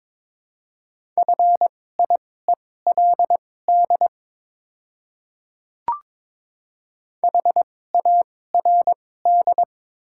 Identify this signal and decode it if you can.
Morse code, heard as a keyed tone, about 22 wpm, spaced as for 16 wpm, 700 Hz. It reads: FIELD HARD
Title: 2x Character Spacing - Top 500 Words - 22wpm